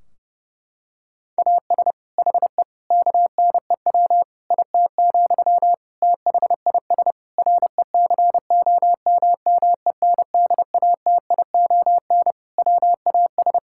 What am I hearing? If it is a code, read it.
Morse code, heard as a keyed tone, about 30 wpm, 700 Hz. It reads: AS HE KNEW IT, THIS RECOMMENDATION WAS
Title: Sentences from Top 1600 Words 30wpm